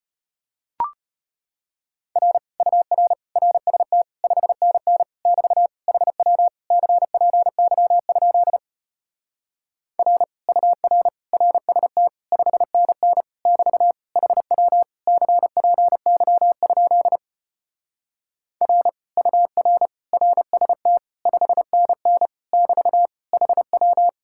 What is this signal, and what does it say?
Morse code, heard as a keyed tone, about 33 wpm, 700 Hz. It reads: R UR RST 5NN <BT> HW CPY? R UR RST 5NN <BT> HW CPY? R UR RST 5NN <BT> HW